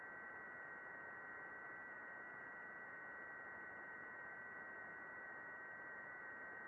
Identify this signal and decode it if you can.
FT8.
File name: contests / june-vhf-k5tr- -0200